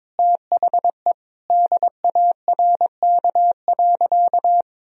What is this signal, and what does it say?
Morse code, heard as a keyed tone, about 22 wpm, 700 Hz. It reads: THE DARK.
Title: Sentences from Top 1000 Words - Review Part 4 - 22wpm